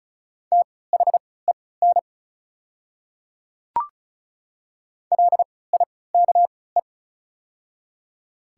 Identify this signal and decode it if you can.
Morse code, heard as a keyed tone, about 36 wpm, spaced as for 20 wpm, 700 Hz. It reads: THEN LIKE
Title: 3x Character Spacing - Top 200 Words - 35wpm